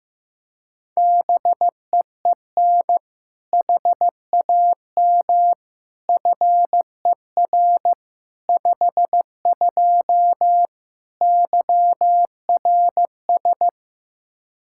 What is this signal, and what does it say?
Morse code, heard as a keyed tone, about 15 wpm, 700 Hz. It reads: BEEN HAM FER 52 YRS
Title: Practice QSO - Exchange 3 - More Info - 15wpm